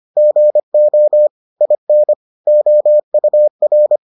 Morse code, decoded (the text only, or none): GO IN OUR